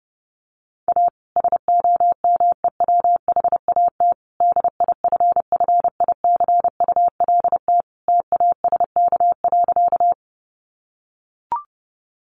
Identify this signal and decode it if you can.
Morse code, heard as a keyed tone, about 30 wpm, 700 Hz. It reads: A SOMEWHAT DIFFICULT TASK.
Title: Sentences from Top 1300 Words 30wpm